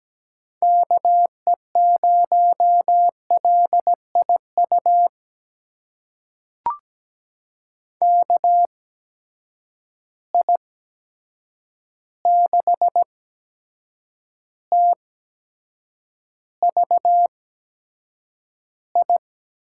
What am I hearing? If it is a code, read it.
Morse code, heard as a keyed tone, about 17 wpm, spaced as for 7 wpm, 700 Hz. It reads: KE0LIU KI6TVI
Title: USA Call Signs - 8x Character Spacing - 17wpm